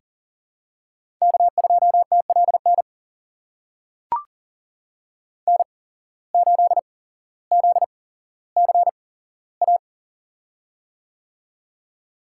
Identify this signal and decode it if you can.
Morse code, heard as a keyed tone, about 40 wpm, spaced as for 17 wpm, 700 Hz. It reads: K2TLN N8ZCA
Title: USA Call Signs - 5 Characters - 8x Character Spacing - 40wpm